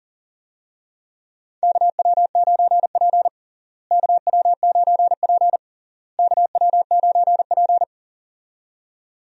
Morse code, heard as a keyed tone, about 40 wpm, 700 Hz. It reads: KW9P KW9P KW9P
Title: USA Call Signs - 4 Characters - Repeated 3x - 40wpm